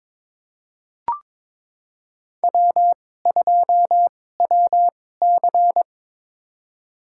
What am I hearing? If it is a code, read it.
Morse code, heard as a keyed tone, about 21 wpm, spaced as for 16 wpm, 700 Hz. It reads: W2WC
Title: USA Call Signs - 4 Characters - 2x Character Spacing - 22wpm